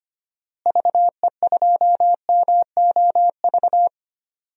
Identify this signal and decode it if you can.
Morse code, heard as a keyed tone, about 25 wpm, 700 Hz. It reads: VE2MOV
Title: DXCC-100 Call Signs - 6 Characters - Repeated 3x - 25wpm